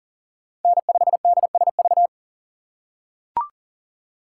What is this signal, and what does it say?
Morse code, heard as a keyed tone, about 40 wpm, 700 Hz. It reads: N5DSV
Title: USA Call Signs - 5 Characters - 3x Character Spacing - 40wpm